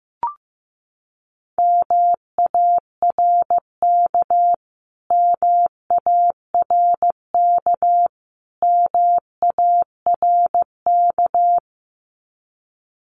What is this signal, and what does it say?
Morse code, heard as a keyed tone, about 15 wpm, 700 Hz. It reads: MARK MARK MARK
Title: Top 500 Words Repeated 15wpm